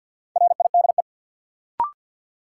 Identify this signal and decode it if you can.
Morse code, heard as a keyed tone, about 52 wpm, 700 Hz. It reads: RIDE
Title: Sentences from Top 1200 Words - New Word Drill Alt - 50wpm